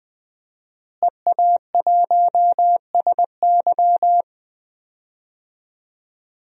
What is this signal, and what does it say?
Morse code, heard as a keyed tone, about 20 wpm, 700 Hz. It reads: EA1SY
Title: DXCC-100 Call Signs - 5 Characters - Repeated 3x - 20wpm